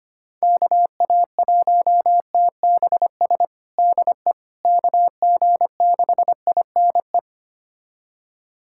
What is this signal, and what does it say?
Morse code, heard as a keyed tone, about 25 wpm, 700 Hz. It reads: KA1TBS DE KG6INE
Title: USA Callsign DE Callsign 25wpm